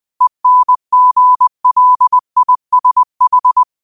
Morse code, heard as a keyed tone, about 20 wpm, 1000 Hz. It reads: ENGLISH
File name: qotdfiles / 20wpm